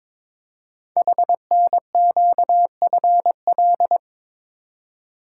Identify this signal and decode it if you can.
Morse code, heard as a keyed tone, about 23 wpm, 700 Hz. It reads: HNQFL
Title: Code Groups - 5 Letters - 22wpm